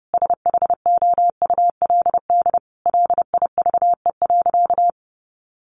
Morse code, heard as a keyed tone, about 30 wpm, 700 Hz. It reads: SHOULD LIVE.
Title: Sentences from Top 1100 Words 30wpm